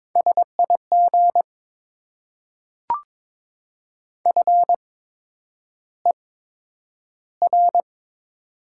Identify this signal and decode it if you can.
Morse code, heard as a keyed tone, about 22 wpm, spaced as for 9 wpm, 700 Hz. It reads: SIG FER
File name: Top 90 QSO Elements - 8x Character Spacing - 22wpm